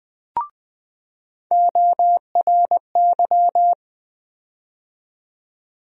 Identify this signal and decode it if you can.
Morse code, heard as a keyed tone, about 20 wpm, 700 Hz. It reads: ORY
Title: USA Call Sign Postfixes - 3 Letters 20wpm